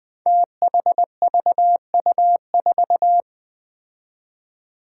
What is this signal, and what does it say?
Morse code, heard as a keyed tone, about 21 wpm, 700 Hz. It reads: THVU4
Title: Five Challenging Letter-Number Combinations 20wpm